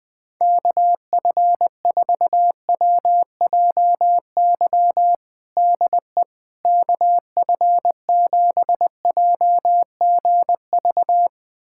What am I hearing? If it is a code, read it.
Morse code, heard as a keyed tone, about 20 wpm, 700 Hz. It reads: KF4WJY DE KF7JGV
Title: USA Callsign DE Callsign 20wpm